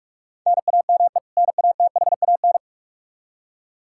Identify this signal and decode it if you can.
Morse code, heard as a keyed tone, about 46 wpm, 700 Hz. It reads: NAME NATHAN
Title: Practice QSO - Exchange 1 - Beginning - 45wpm